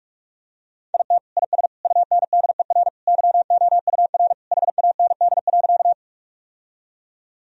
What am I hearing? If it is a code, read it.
Morse code, heard as a keyed tone, about 46 wpm, 700 Hz. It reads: IT IS UNDER YOUR HAND.
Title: Sentences from Top 1000 Words - Review Part 1 - 45wpm